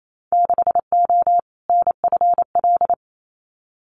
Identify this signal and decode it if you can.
Morse code, heard as a keyed tone, about 28 wpm, 700 Hz. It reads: 6O NFL
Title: Winter Field Day Exchanges 28wpm